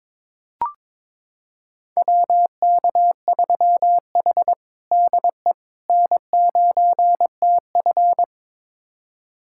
Morse code, heard as a keyed tone, about 22 wpm, 700 Hz. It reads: WK3H DE N9TF
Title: USA Callsign DE Callsign - 4 Characters - 22wpm